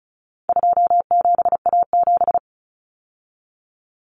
Morse code, heard as a keyed tone, about 34 wpm, 700 Hz. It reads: 27A7
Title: Common Cut Numbers - 4-Digit 35wpm